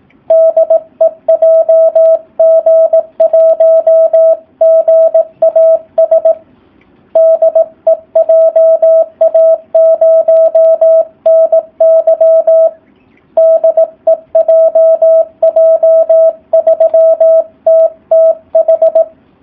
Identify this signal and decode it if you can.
Morse code, heard as a keyed tone, about 18 wpm, 650 Hz, with radio noise. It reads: DE JG1GAS DE JA0NY DE JJ3TTH
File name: jh8cbh / 2019cw / SP04- w